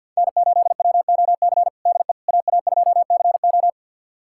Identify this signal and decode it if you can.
Morse code, heard as a keyed tone, about 52 wpm, 700 Hz. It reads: N8WOC DE AA2CO
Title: USA Callsign DE Callsign - 5 Characters - 50wpm